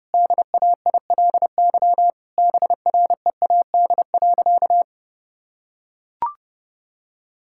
Morse code, heard as a keyed tone, about 30 wpm, 700 Hz. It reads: DAILY BREAD.
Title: Sentences from Top 1600 Words 30wpm